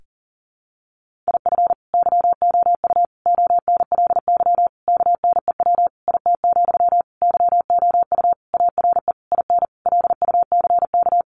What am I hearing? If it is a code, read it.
Morse code, heard as a keyed tone, about 40 wpm, 700 Hz. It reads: IF YOU ONLY KNEW IT, YOU ARE IN LUCK
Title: Sentences from Top 1500 Words 40wpm